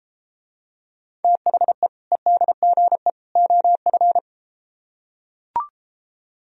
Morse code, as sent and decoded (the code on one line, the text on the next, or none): - .... / . -.. --. / --- ..-.
THE EDGE OF